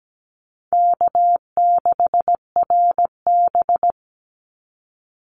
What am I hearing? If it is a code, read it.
Morse code, heard as a keyed tone, about 17 wpm, 700 Hz. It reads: K6RB
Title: Call Signs Repeated 2x - 17wpm